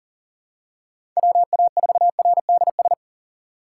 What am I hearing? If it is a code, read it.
Morse code, heard as a keyed tone, about 40 wpm, 700 Hz. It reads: WA4RDS
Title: USA Call Signs - 6 Characters - 3x Character Spacing - 40wpm